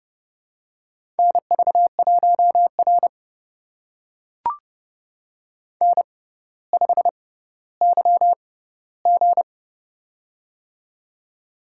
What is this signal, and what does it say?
Morse code, heard as a keyed tone, about 29 wpm, spaced as for 15 wpm, 700 Hz. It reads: NV1R N5YG
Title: USA Call Signs - 4 Characters - 6x Character Spacing - 30wpm